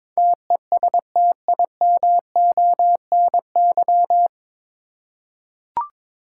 Morse code, heard as a keyed tone, about 22 wpm, 700 Hz. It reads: TESTIMONY
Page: Morse Code Ninja practice 22wpm